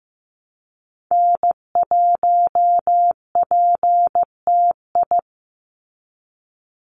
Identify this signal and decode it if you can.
Morse code, heard as a keyed tone, about 15 wpm, 700 Hz. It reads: N1PTI
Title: USA Call Signs 15wpm